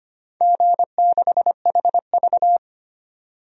Morse code, heard as a keyed tone, about 25 wpm, 700 Hz. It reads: G6HV
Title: DXCC-100 Call Signs - 4 Characters - 25wpm